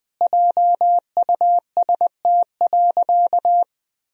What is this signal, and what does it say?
Morse code, heard as a keyed tone, about 20 wpm, 700 Hz. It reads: JUST.
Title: Sentences from Top 1000 Words - Review Part 4 - 20wpm